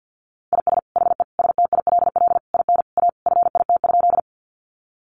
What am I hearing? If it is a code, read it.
Morse code, heard as a keyed tone, about 50 wpm, 700 Hz. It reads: IS HE STILL IN A FIT?
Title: Sentences from Top 800 Words 50wpm